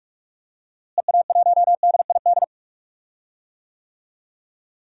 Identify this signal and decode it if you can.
Morse code, heard as a keyed tone, about 46 wpm, 700 Hz. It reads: EA1DID
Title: DXCC-100 Call Signs - 6 Characters - Repeated 2x - 45wpm